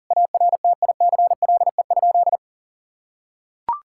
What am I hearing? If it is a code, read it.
Morse code, heard as a keyed tone, about 37 wpm, 700 Hz. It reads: ARTICLE?
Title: Sentences from Top 1000 Words - Review Part 5 - 40wpm